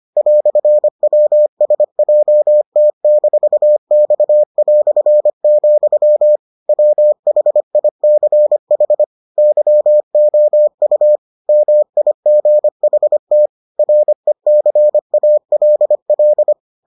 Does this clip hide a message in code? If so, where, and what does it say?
Yes: Morse code from 0.2 to 16.5 s, heard as a keyed tone, about 25 wpm, 600 Hz. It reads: "WSJT-X", WHICH YOU MIGHT RECALL